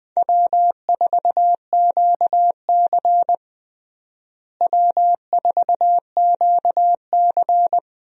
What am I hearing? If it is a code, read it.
Morse code, heard as a keyed tone, about 21 wpm, 700 Hz. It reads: W4QC W4QC
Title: USA Call Signs - 4 Characters - Repeated 2x - 20wpm